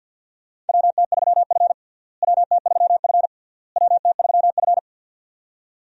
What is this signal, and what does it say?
Morse code, heard as a keyed tone, about 52 wpm, 700 Hz. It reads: WT3F WT3F WT3F
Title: USA Call Signs - 4 Characters - Repeated 3x - 50wpm